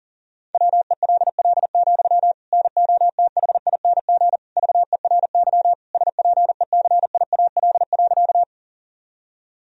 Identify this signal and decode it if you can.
Morse code, heard as a keyed tone, about 37 wpm, 700 Hz. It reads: WELL, NOTHING VERY SPECIAL.
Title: Sentences from Top 1000 Words - Review Part 3 - 40wpm